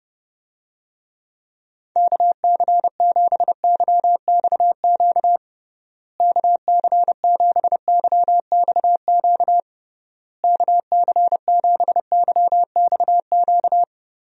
Morse code, heard as a keyed tone, about 29 wpm, 700 Hz. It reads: KC7YXQ KC7YXQ KC7YXQ
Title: USA Call Signs - 6 Characters - Repeated 3x - 30wpm